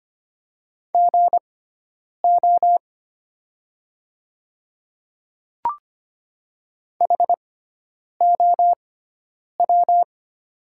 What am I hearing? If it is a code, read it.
Morse code, heard as a keyed tone, about 25 wpm, spaced as for 9 wpm, 700 Hz. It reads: GO HOW
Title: 6x Character Spacing - Top 100 Words - 25wpm